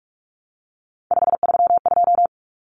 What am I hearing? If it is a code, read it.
Morse code, heard as a keyed tone, about 44 wpm, 700 Hz. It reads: 532
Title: Numbers - 3-Digit 45wpm